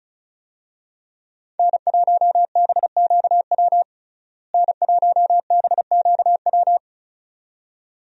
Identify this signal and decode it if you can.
Morse code, heard as a keyed tone, about 35 wpm, 700 Hz. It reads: N1BQW N1BQW
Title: USA Call Signs - 5 Characters - Repeated 2x - 35wpm